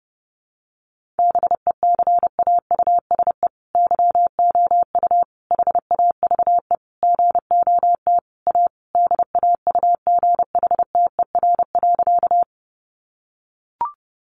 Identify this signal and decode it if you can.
Morse code, heard as a keyed tone, about 30 wpm, 700 Hz. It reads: BECAUSE YOU HAVE GOT A DAUGHTER.
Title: Sentences from Top 1500 Words 30wpm